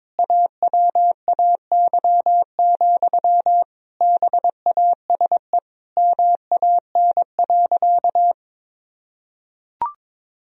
Morse code, heard as a keyed tone, about 22 wpm, 700 Hz. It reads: AWAY, BASE MAN.
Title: Sentences from Top 1000 Words - Review Part 1 - 22wpm